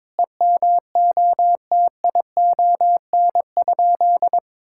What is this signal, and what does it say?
Morse code, heard as a keyed tone, about 22 wpm, 700 Hz. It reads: EMOTION?
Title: Sentences from Top 1600 Words 22wpm